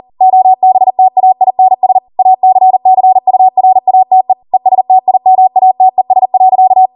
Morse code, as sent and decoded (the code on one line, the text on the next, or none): --- -... - .- .. -. ... / .- -.-. -.-. ..- .-. .- - . / . ... - .. -- .- - . ... .-.-.-
OBTAINS ACCURATE ESTIMATES.